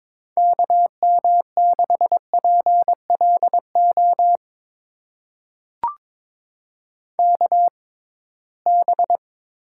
Morse code, heard as a keyed tone, about 22 wpm, spaced as for 17 wpm, 700 Hz. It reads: KM6PLO KB
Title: USA Call Signs - 6 Characters - 6x Character Spacing - 22wpm